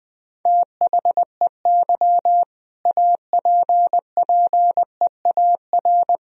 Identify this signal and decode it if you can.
Morse code, heard as a keyed tone, about 20 wpm, 700 Hz. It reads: THEY APPEAR
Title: Sentences from Top 1000 Words - Review Part 2 - 20wpm